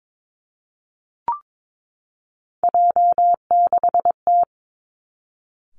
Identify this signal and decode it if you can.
Morse code, heard as a keyed tone, about 22 wpm, 700 Hz. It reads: J6T
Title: Three Challenging Letter-Number Combinations 22wpm